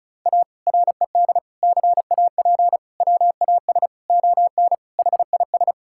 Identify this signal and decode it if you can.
Morse code, heard as a keyed tone, about 34 wpm, 700 Hz. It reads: A RED CAP WAS ON HIS
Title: Sentences from Top 1600 Words 35wpm